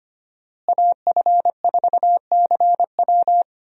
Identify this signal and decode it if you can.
Morse code, heard as a keyed tone, about 25 wpm, 700 Hz. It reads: AF4CW
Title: USA Call Signs - 5 Characters - Repeated 2x - 25wpm